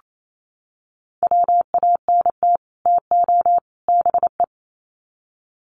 Morse code, heard as a keyed tone, about 28 wpm, 700 Hz. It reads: WANT TO BE
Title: Sets of 3 Words 28wpm